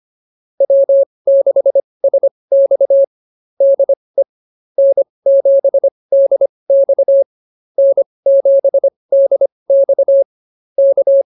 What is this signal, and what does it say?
Morse code, heard as a keyed tone, about 24 wpm, spaced as for 20 wpm, 560 Hz. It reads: W6SX DE N7DX N7DX K
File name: QSO102 20